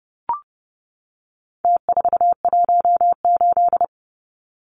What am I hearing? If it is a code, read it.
Morse code, heard as a keyed tone, about 30 wpm, 700 Hz. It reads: T418
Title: Numbers - Contest Style - 4-Digit 30wpm